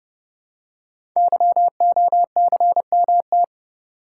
Morse code, heard as a keyed tone, about 28 wpm, 700 Hz. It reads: YOCMT
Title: Code Groups - 5 Letters - 30wpm